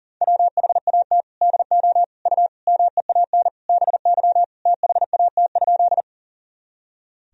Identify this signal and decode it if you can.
Morse code, heard as a keyed tone, about 40 wpm, 700 Hz. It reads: WHAT DO U MEAN BY THAT?
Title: Sentences from Top 200 Words and Abbreviations 40wpm